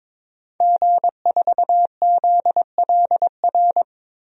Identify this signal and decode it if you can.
Morse code, heard as a keyed tone, about 22 wpm, 700 Hz. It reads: G4ZLR
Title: DXCC-100 Call Signs - 5 Characters - 22wpm